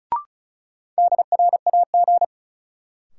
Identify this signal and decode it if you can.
Morse code, heard as a keyed tone, about 36 wpm, 700 Hz. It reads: DRAG
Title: Sentences from Top 1500 Words - New Word Drill Alt - 35wpm